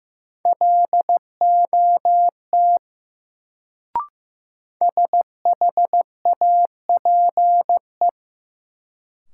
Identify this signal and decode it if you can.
Morse code, heard as a keyed tone, about 15 wpm, 700 Hz. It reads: LOT SHAPE
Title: Sentences from Top 500 Words - New Word Drill - 15wpm